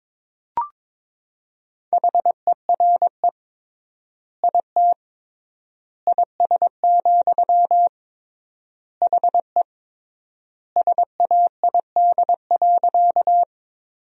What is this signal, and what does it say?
Morse code, heard as a keyed tone, about 22 wpm, 700 Hz. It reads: HERE IT IS, HE SAID.
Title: 3x Word-Spacing - Sentences from Top 200 Words - 22wpm